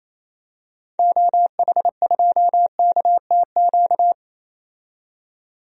Morse code, heard as a keyed tone, about 28 wpm, 700 Hz. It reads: OH2KTQ
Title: DXCC-100 Call Signs - 6 Characters - 28wpm